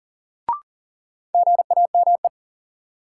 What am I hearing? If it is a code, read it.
Morse code, heard as a keyed tone, about 37 wpm, 700 Hz. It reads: GAME
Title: Sentences from Top 500 Words - New Word Drill - 40wpm